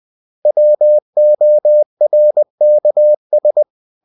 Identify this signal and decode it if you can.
Morse code, heard as a keyed tone, about 20 wpm, 600 Hz. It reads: WORKS
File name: cw-books / winnie-the-pooh / winnie-20wpm-page-0114